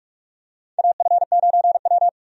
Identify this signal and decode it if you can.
Morse code, heard as a keyed tone, about 46 wpm, 700 Hz. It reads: AF9W
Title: USA Call Signs - 4 Characters - Repeated 2x - 45wpm